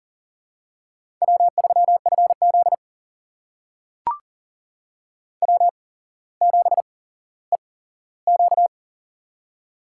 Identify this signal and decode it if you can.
Morse code, heard as a keyed tone, about 39 wpm, spaced as for 17 wpm, 700 Hz. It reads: W3FZ W7EQ
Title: USA Call Signs - 4 Characters - 8x Character Spacing - 40wpm